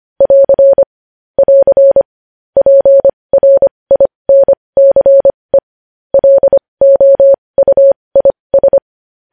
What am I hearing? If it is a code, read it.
Morse code, heard as a keyed tone, about 25 wpm, spaced as for 20 wpm, 550 Hz. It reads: <AR> <AR> PRINCE LOUIS
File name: podcast / News headlines-1x-25x20